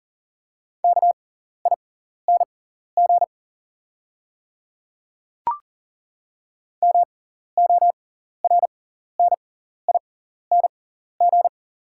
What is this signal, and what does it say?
Morse code, heard as a keyed tone, about 39 wpm, spaced as for 14 wpm, 690 Hz. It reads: KING MORNING
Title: 6x Character Spacing - Top 500 Words - 40wpm